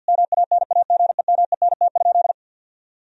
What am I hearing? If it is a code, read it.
Morse code, heard as a keyed tone, about 50 wpm, 700 Hz. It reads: MANAGEMENT?
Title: Sentences from Top 1000 Words - Review Part 4 - 50wpm